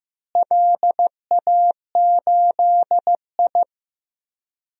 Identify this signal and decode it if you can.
Morse code, heard as a keyed tone, about 15 wpm, 700 Hz. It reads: LA8I
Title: DXCC-100 Call Signs - 4 Characters - Repeated 2x - 15wpm